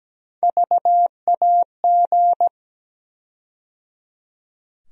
Morse code, heard as a keyed tone, about 17 wpm, 700 Hz. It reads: VAG
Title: USA Call Sign Postfixes - 3 Letters 17wpm